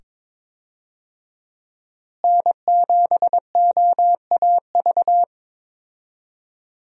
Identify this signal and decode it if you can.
Morse code, heard as a keyed tone, about 22 wpm, 700 Hz. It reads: N7OAV
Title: USA Call Signs - 5 Characters - Repeated 2x - 22wpm